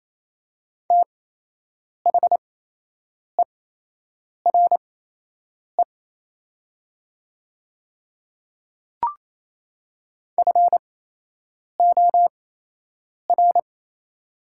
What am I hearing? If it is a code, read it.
Morse code, heard as a keyed tone, about 28 wpm, spaced as for 8 wpm, 690 Hz. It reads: THERE FOR
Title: 8x Character Spacing - Top 100 Words - 28wpm